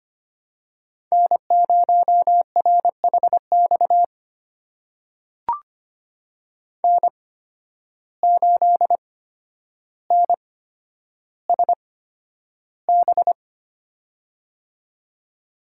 Morse code, heard as a keyed tone, about 25 wpm, spaced as for 11 wpm, 700 Hz. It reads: N0RHX N8NSB
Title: USA Call Signs - 5 Characters - 8x Character Spacing - 25wpm